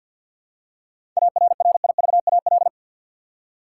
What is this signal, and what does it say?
Morse code, heard as a keyed tone, about 50 wpm, 700 Hz. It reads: ARRIVAL
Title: Seven Letter Words - 50wpm